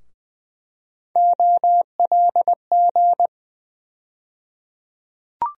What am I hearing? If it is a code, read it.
Morse code, heard as a keyed tone, about 20 wpm, 700 Hz. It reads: OLG
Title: USA Call Sign Postfixes - 3 Letters 20wpm